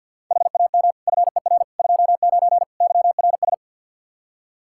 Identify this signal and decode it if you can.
Morse code, heard as a keyed tone, about 51 wpm, 700 Hz. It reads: HAM FER 29 YRS
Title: Practice QSO - Exchange 3 - More Info - 50wpm